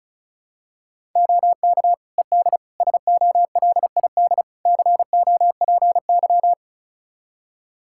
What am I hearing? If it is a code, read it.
Morse code, heard as a keyed tone, about 36 wpm, 700 Hz. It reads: OK ED SOLID COPY
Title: K1USN SST QSO Practice 35wpm